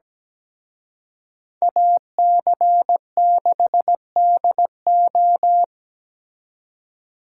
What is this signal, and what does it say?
Morse code, heard as a keyed tone, about 20 wpm, 700 Hz. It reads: AC6DO
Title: Speed Racing - USA 2x2 Call Signs 17wpm